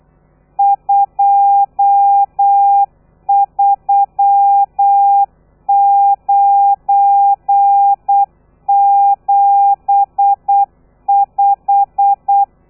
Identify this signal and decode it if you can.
Morse code, heard as a keyed tone, about 8 wpm, 800 Hz. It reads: 23975
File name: telegrafia / i2tfj / Lez12 35